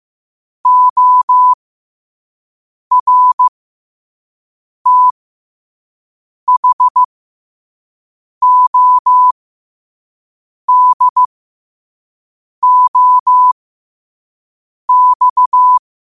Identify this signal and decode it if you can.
Morse code, heard as a keyed tone, about 15 wpm, spaced as for 5 wpm, 1000 Hz. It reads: ORTHODOX